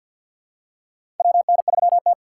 Morse code, heard as a keyed tone, about 50 wpm, 700 Hz. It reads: WN3T
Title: USA Call Signs - 4 Characters - 6x Character Spacing - 50wpm